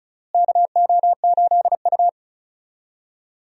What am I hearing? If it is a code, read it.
Morse code, heard as a keyed tone, about 36 wpm, 700 Hz. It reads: KO8U
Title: USA Call Signs - 4 Characters - Repeated 3x - 35wpm